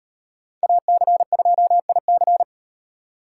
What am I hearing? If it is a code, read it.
Morse code, heard as a keyed tone, about 38 wpm, 700 Hz. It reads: AC2IC